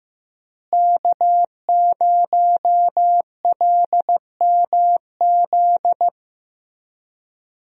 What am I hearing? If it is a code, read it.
Morse code, heard as a keyed tone, about 16 wpm, 700 Hz. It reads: K0LMZ
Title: USA Call Signs - 5 Characters - Repeated 3x - 15wpm